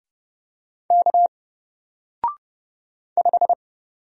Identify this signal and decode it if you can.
Morse code, heard as a keyed tone, about 30 wpm, 700 Hz. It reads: K 5
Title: Single Character (Letter:Number:Punctuation:Prosign) 30wpm